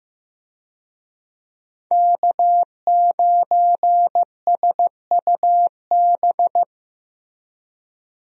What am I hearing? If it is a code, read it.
Morse code, heard as a keyed tone, about 15 wpm, 700 Hz. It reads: K9SUB
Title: USA Call Signs - 5 Characters - Repeated 2x - 15wpm